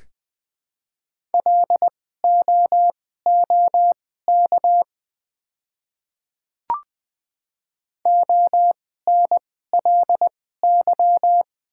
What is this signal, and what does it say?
Morse code, heard as a keyed tone, about 20 wpm, spaced as for 14 wpm, 700 Hz. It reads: LOOK ONLY
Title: 2x Character Spacing - Top 200 Words - 20wpm